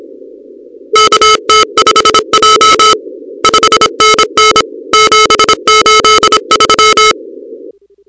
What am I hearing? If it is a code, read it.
Morse code, heard as a keyed tone, about 27 wpm, 1250 Hz, with noise. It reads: KT5J 5NN 783
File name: contests / 2019-cw-wpx-kt5j-k5pi- -2330 0-38